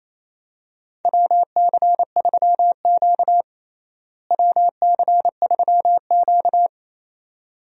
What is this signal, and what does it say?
Morse code, heard as a keyed tone, about 28 wpm, 700 Hz. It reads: WC3Q WC3Q
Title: USA Call Signs - 4 Characters - Repeated 2x - 28wpm